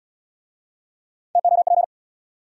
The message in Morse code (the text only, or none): TOP